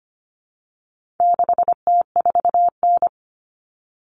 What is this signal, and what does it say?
Morse code, heard as a keyed tone, about 25 wpm, 700 Hz. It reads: 6T4N
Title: Common Cut Numbers - 4-Digit 25wpm